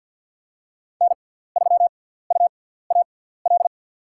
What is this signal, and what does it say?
Morse code, heard as a keyed tone, about 50 wpm, spaced as for 17 wpm, 700 Hz. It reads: N3UAL
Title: USA Call Signs - 6x Character Spacing - 50wpm